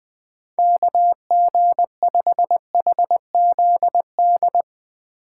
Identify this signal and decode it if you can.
Morse code, heard as a keyed tone, about 20 wpm, 700 Hz. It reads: KG5HZD
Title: USA Call Signs - 6 Characters - 6x Character Spacing - 20wpm